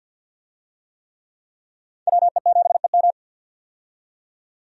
Morse code, heard as a keyed tone, about 47 wpm, 700 Hz. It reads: WE7EM
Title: USA Call Signs - 5 Characters - 50wpm